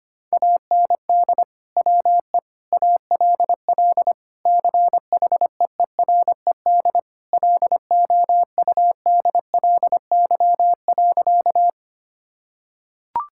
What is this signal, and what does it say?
Morse code, heard as a keyed tone, about 25 wpm, 700 Hz. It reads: AND WE ALL CHEERED LOUDLY.
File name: practice / 25wpm / Sentences with 2-Word Phrases - 25wpm